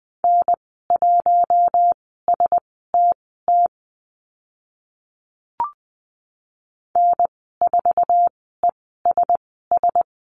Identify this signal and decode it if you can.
Morse code, heard as a keyed tone, about 20 wpm, spaced as for 14 wpm, 700 Hz. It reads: N1STT N4ESS
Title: 055 Lesson 10g - Letter r - Call Signs 20wpm